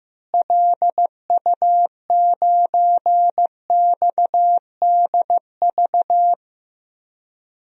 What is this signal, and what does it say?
Morse code, heard as a keyed tone, about 16 wpm, 700 Hz. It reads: LU9XDV
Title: DXCC-100 Call Signs - 6 Characters - Repeated 3x - 15wpm